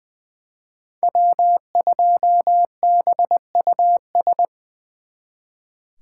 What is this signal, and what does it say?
Morse code, heard as a keyed tone, about 23 wpm, 700 Hz. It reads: W2BUS
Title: Speed Racing - USA 1x3 Call Signs 20wpm